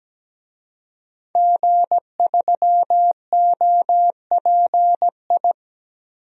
Morse code, heard as a keyed tone, about 17 wpm, 700 Hz. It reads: G3OPI
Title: DXCC-100 Call Signs - 5 Characters - 17wpm